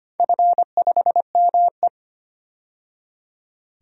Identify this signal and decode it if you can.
Morse code, heard as a keyed tone, about 25 wpm, 700 Hz. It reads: F5ME
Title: DXCC-100 Call Signs - 4 Characters - 25wpm